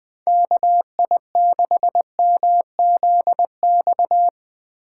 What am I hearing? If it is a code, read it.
Morse code, heard as a keyed tone, about 20 wpm, 700 Hz. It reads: KI6MZX